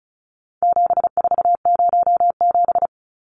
Morse code, heard as a keyed tone, about 34 wpm, 700 Hz. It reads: 7407
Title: Numbers - 4-Digit 35wpm